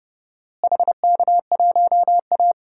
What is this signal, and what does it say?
Morse code, heard as a keyed tone, about 32 wpm, 700 Hz. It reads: HK1A
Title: DXCC-100 Call Signs - 4 Characters - Repeated 2x - 30wpm